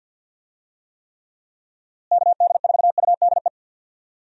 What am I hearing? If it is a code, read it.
Morse code, heard as a keyed tone, about 52 wpm, 700 Hz. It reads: KD4UDE
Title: USA Call Signs - 6 Characters - Repeated 3x - 50wpm